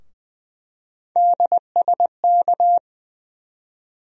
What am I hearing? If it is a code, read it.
Morse code, heard as a keyed tone, about 20 wpm, 700 Hz. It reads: DSK
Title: USA Call Sign Postfixes - 3 Letters 20wpm